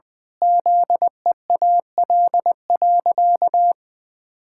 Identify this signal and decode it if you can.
Morse code, heard as a keyed tone, about 19 wpm, 700 Hz. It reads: ZEAL.
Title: Pangram Challenge v2 - 20wpm